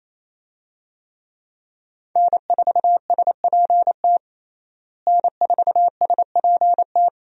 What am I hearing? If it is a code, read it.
Morse code, heard as a keyed tone, about 28 wpm, 700 Hz. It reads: N4SPT N4SPT
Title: USA Call Signs - 5 Characters - Repeated 2x - 28wpm